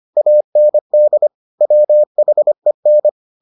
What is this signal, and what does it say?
Morse code, heard as a keyed tone, about 25 wpm, 600 Hz. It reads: AND WHEN